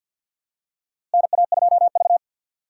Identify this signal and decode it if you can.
Morse code, heard as a keyed tone, about 50 wpm, 700 Hz. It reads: NA2V
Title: USA Call Signs - 4 Characters - Repeated 3x - 50wpm